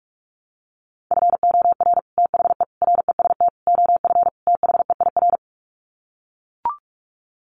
Morse code, heard as a keyed tone, about 44 wpm, 700 Hz. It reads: FOR THE REST OF THEIR
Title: Sets of 5 Words Encore2 45wpm